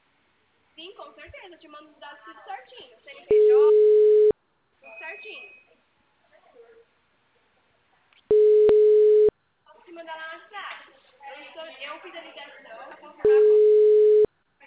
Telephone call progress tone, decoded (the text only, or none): ringback tone